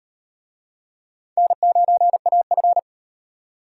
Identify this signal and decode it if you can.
Morse code, heard as a keyed tone, about 34 wpm, 700 Hz. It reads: N9AF